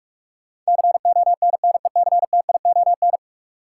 Morse code, heard as a keyed tone, about 46 wpm, 700 Hz. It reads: CONNECTION
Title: Sentences from Top 1300 Words - New Word Drill - 45wpm